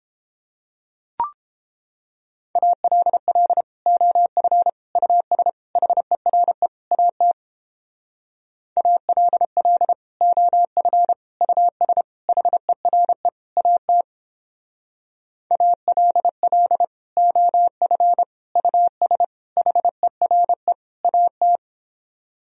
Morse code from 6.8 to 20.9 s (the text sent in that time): AT ALL OF US HERE AT ALL OF US HERE